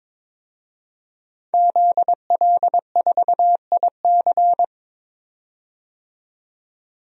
Morse code, heard as a keyed tone, about 22 wpm, 700 Hz. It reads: ZL4IC
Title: DXCC-100 Call Signs - 5 Characters - 22wpm